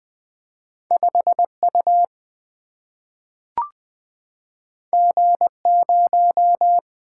Morse code, heard as a keyed tone, about 20 wpm, 700 Hz. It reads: 5 U G 0
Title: Single-Letter Single-Number 20wpm